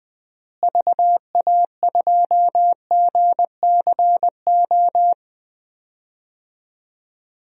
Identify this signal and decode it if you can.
Morse code, heard as a keyed tone, about 21 wpm, 700 Hz. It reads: VA2GCO
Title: DXCC-100 Call Signs - 6 Characters - Repeated 2x - 20wpm